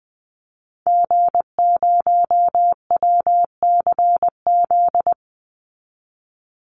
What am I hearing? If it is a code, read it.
Morse code, heard as a keyed tone, about 20 wpm, 700 Hz. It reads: G0WCZ
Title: Call Signs Repeated 3x - 20wpm